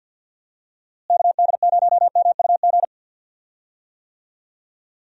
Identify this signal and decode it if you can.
Morse code, heard as a keyed tone, about 51 wpm, 700 Hz. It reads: KD0MUG
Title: USA Call Signs - 6 Characters - Repeated 3x - 50wpm